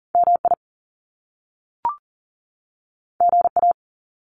Morse code, heard as a keyed tone, about 38 wpm, 700 Hz. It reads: MI GA
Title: US States 40wpm